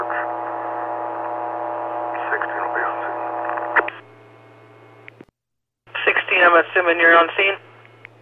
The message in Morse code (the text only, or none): none